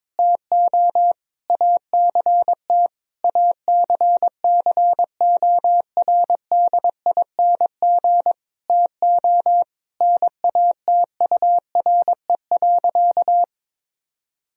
Morse code, heard as a keyed tone, about 22 wpm, 700 Hz. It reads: TO ACT ACCORDING TO NATURE.
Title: Sentences from Top 1000 Words - Review Part 4 - 22wpm